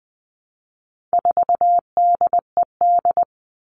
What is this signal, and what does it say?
Morse code, heard as a keyed tone, about 20 wpm, 700 Hz. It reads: 4DED
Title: All Cut Numbers - 4-Digit 20wpm